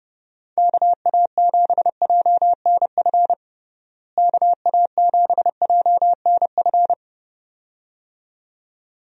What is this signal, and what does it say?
Morse code, heard as a keyed tone, about 32 wpm, 700 Hz. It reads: KA7JNF KA7JNF
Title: USA Call Signs - 6 Characters - Repeated 2x - 30wpm